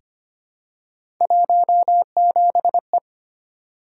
Morse code, heard as a keyed tone, about 25 wpm, 700 Hz. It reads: 17E
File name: Three Letter-Number Combinations - 25wpm